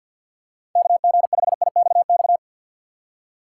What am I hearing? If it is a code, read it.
Morse code, heard as a keyed tone, about 49 wpm, 700 Hz. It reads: KG5IXX
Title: USA Call Signs - 6 Characters - 6x Character Spacing - 50wpm